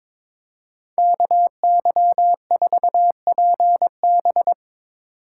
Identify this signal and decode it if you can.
Morse code, heard as a keyed tone, about 24 wpm, 700 Hz. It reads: KY4PB
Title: Speed Racing - USA 2x2 Call Signs 22wpm